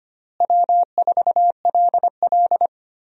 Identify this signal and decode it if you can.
Morse code, heard as a keyed tone, about 25 wpm, 700 Hz. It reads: W4LL
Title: USA Call Signs - 4 Characters - Repeated 3x - 25wpm